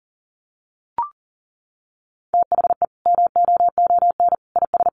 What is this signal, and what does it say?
Morse code, heard as a keyed tone, about 40 wpm, 700 Hz. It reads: THE MOON IS
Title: Sentences from Top 1200 Words 40wpm